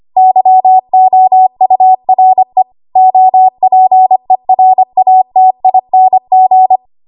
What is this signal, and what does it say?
Morse code, heard as a keyed tone, about 25 wpm, 750 Hz. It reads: YOURE OPERATING
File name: Archive / 25 WPM / 25WPM